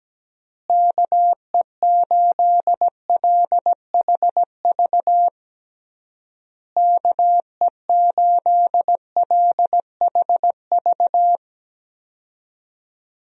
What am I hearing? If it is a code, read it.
Morse code, heard as a keyed tone, about 17 wpm, 700 Hz. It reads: KE8LHV KE8LHV
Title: USA Call Signs - 6 Characters - Repeated 3x - 17wpm